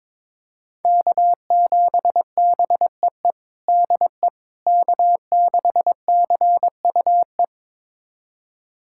Morse code, heard as a keyed tone, about 22 wpm, 700 Hz. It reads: K7BEE DE K6CUE
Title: USA Callsign DE Callsign - 5 Characters - 22wpm